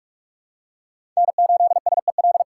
Morse code, heard as a keyed tone, about 43 wpm, 700 Hz. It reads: N8SEL
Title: USA Call Signs - 5 Characters - 4x Character Spacing - 45wpm